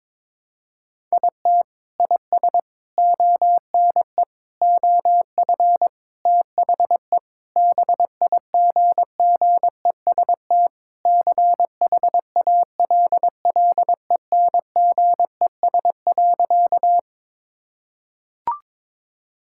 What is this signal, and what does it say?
Morse code, heard as a keyed tone, about 22 wpm, 700 Hz. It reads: IT IS ONE OF THE BIGGEST CHALLENGES.
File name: practice / 22wpm / Sentences with 4-Word Phrases - 22wpm